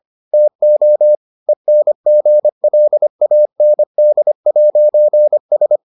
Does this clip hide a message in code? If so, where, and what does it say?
Yes: Morse code from 0.3 to 5.8 s, heard as a keyed tone, about 25 wpm, 600 Hz. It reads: TO ENGLAND'S